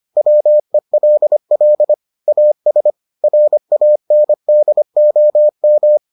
Morse code, heard as a keyed tone, about 25 wpm, 600 Hz. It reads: WELL AS RANDOM